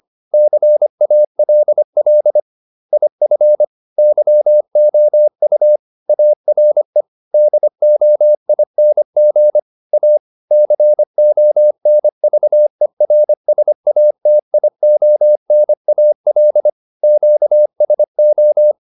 Morse code, heard as a keyed tone, about 25 wpm, 600 Hz. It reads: CALL IF YOU ARE DOING A CONVERSATIONAL QSO